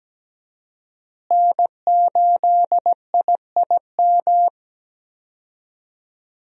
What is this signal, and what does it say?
Morse code, heard as a keyed tone, about 17 wpm, 700 Hz. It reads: N8IIM
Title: USA Call Signs - 5 Characters - Repeated 2x - 17wpm